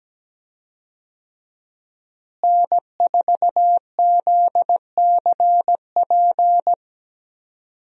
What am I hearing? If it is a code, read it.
Morse code, heard as a keyed tone, about 17 wpm, 700 Hz. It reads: N4ZCP
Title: USA Call Signs - 5 Characters - Repeated 2x - 17wpm